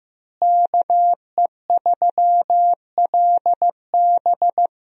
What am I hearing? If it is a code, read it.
Morse code, heard as a keyed tone, about 15 wpm, 700 Hz. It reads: KE3LB
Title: USA Call Signs - 5 Characters - Repeated 3x - 15wpm